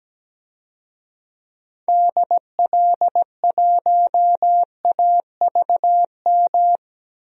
Morse code, heard as a keyed tone, about 17 wpm, 700 Hz. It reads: DL1AVM
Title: DXCC-100 Call Signs - 6 Characters - 17wpm